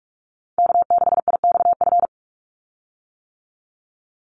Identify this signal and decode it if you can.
Morse code, heard as a keyed tone, about 44 wpm, 700 Hz. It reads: K6IXF
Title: USA Call Signs 45wpm